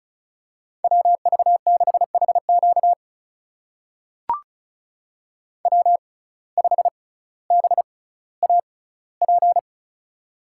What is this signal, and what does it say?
Morse code, heard as a keyed tone, about 36 wpm, spaced as for 18 wpm, 700 Hz. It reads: WV6HQ W5BAP
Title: USA Call Signs - 5 Characters - 6x Character Spacing - 35wpm